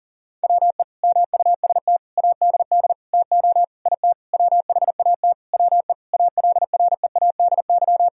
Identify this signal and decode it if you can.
Morse code, heard as a keyed tone, about 37 wpm, 700 Hz. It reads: WE MUST ADD TO IT WHAT WE ALREADY
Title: Sentences from Top 1000 Words - Review Part 1 - 40wpm